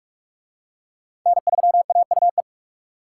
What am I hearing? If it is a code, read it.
Morse code, heard as a keyed tone, about 40 wpm, 700 Hz. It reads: N3AUE